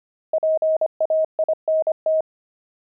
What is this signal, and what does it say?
Morse code, heard as a keyed tone, about 25 wpm, 630 Hz. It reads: PAINT
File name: pangram-fs-25-cs-25-oxide